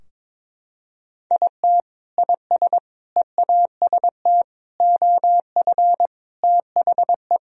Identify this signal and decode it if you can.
Morse code, heard as a keyed tone, about 22 wpm, 700 Hz. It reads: IT IS EAST OF THE
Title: Sentences from Top 1000 Words - Review Part 3 - 22wpm